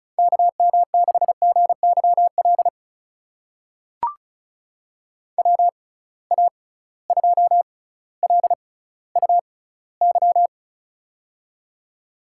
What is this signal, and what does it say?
Morse code, heard as a keyed tone, about 34 wpm, spaced as for 18 wpm, 700 Hz. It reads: KM6GYL WA2LUY
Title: USA Call Signs - 6 Characters - 6x Character Spacing - 35wpm